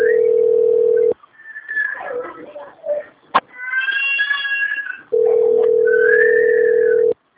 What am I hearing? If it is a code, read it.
Ringback tone (North America), 440 + 480 Hz.
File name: rg-1600-unknown- - - .52713